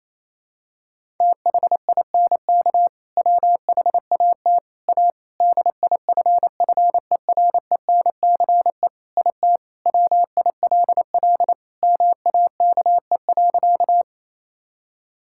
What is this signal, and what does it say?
Morse code, heard as a keyed tone, about 28 wpm, 700 Hz. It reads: THINK WHAT A DIFFERENCE IT WILL MAKE.
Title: Sentences from Top 1000 Words - Review Part 5 - 28wpm